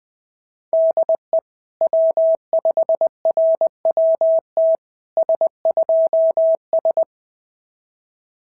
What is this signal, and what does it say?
Morse code, heard as a keyed tone, about 20 wpm, 650 Hz. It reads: DE W5RWT S2S
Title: SOTA QSO Practice 20wpm